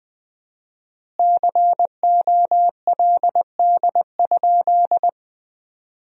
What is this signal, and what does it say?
Morse code, heard as a keyed tone, about 20 wpm, 700 Hz. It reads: COLD?
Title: 4x Word-Spacing - Sentences from Top 400 Words - 20wpm